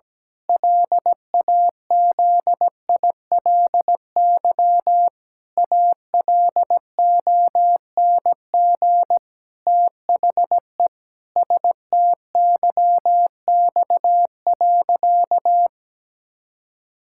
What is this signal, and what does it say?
Morse code, heard as a keyed tone, about 17 wpm, 700 Hz. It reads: LAZILY ALONG THE STYX.
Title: Pangram Challenge v2 - 17wpm